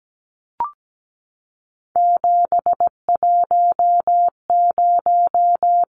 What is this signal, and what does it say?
Morse code, heard as a keyed tone, about 17 wpm, 700 Hz. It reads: 710
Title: Numbers - 3-Digit 17wpm